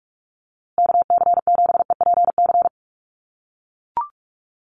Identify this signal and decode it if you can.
Morse code, heard as a keyed tone, about 44 wpm, 700 Hz. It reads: KC7EPC
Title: USA Call Signs Repeated 3x 45wpm